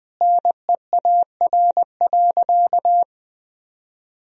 Morse code, heard as a keyed tone, about 21 wpm, 700 Hz. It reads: NEAR.
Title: Sentences from Top 300 Words and Abbreviations 20wpm